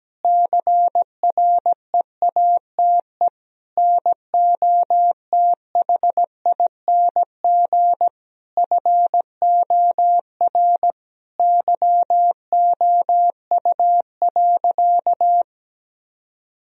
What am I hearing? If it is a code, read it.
Morse code, heard as a keyed tone, about 17 wpm, 700 Hz. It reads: CREATE NOTHING FOR YOU.
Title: Sentences from Top 1000 Words - Review Part 3 - 17wpm